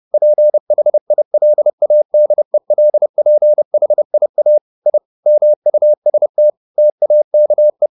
Morse code, heard as a keyed tone, about 30 wpm, 600 Hz. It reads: PHILADELPHIA I MUST TAKE
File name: the old man and the sea / the old man and the sea 1 30wpm